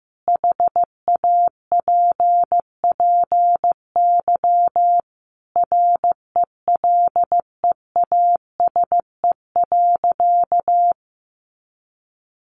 Morse code, heard as a keyed tone, about 15 wpm, 700 Hz. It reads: HAPPY RELEASE.
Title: Sentences from Top 1100 Words 15wpm